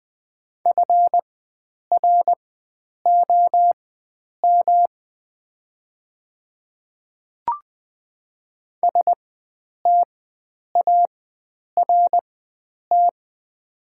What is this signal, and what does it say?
Morse code, heard as a keyed tone, about 20 wpm, spaced as for 9 wpm, 700 Hz. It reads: FROM START
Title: 4x Character Spacing - Top 500 Words - 20wpm